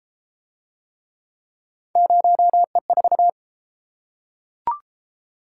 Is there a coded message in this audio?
Yes: Morse code from 2.0 to 3.3 s, heard as a keyed tone, about 30 wpm, 700 Hz. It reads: 0E4